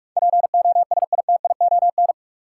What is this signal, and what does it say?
Morse code, heard as a keyed tone, about 45 wpm, 700 Hz. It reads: POSITION
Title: Sentences from Top 900 Words - New Word Drill - 45wpm